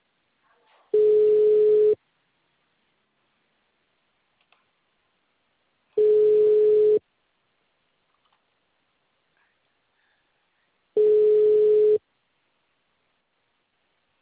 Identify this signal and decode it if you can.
Ringback tone (Europe, 425 Hz).